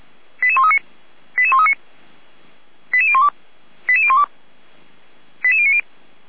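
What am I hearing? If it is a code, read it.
Five-tone selective call (ZVEI), heard as quick runs of five tones, tones of about 70 ms. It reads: 88139 88139 88012 88012 88090
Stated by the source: ZVEI five-tone selective call received by SDR.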